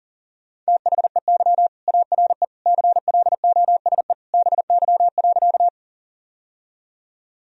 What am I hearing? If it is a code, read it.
Morse code, heard as a keyed tone, about 37 wpm, 700 Hz. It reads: THEY ARE CLOSE BY.
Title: Sentences from Top 1000 Words - Review Part 1 - 40wpm